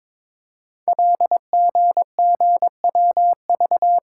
Morse code, heard as a keyed tone, about 23 wpm, 690 Hz. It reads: LGGWV
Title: Code Groups - 5 Letters - 22wpm